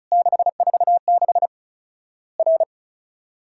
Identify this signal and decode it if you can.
Morse code, heard as a keyed tone, about 34 wpm, 700 Hz. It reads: 646 R
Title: Worked All Europe DX Contest - QTC Exchange - 35wpm